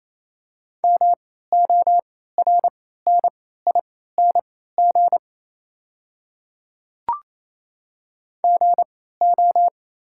Morse code, heard as a keyed tone, about 27 wpm, spaced as for 16 wpm, 700 Hz. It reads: MORNING GO
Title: 3x Character Spacing - Top 500 Words - 28wpm